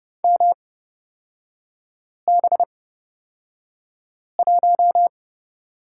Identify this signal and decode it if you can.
Morse code, heard as a keyed tone, about 30 wpm, spaced as for 5 wpm, 690 Hz. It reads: MB1
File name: Lesson 37e - Letter x - Three Characters 30wpm